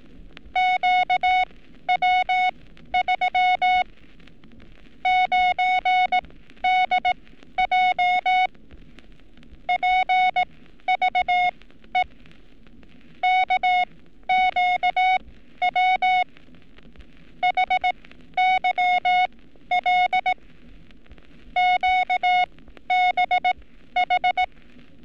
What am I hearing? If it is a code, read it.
Morse code, heard as a keyed tone, about 18 wpm, spaced as for 12 wpm, 730 Hz. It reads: QW3 9DJ PVE KQW HYL QBH
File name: training / NAVPERS 11224-11235 / NAVPERS 11227 RA RB